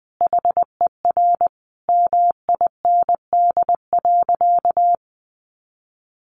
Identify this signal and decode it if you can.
Morse code, heard as a keyed tone, about 20 wpm, 700 Hz. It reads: HER MIND.
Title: Sentences from Top 1300 Words 20wpm